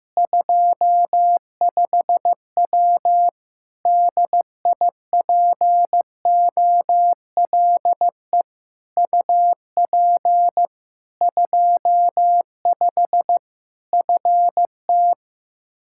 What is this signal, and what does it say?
Morse code, heard as a keyed tone, about 16 wpm, 700 Hz. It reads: 25W DIPOLE UP 25 FT
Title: K1USN SST QSO Practice 15wpm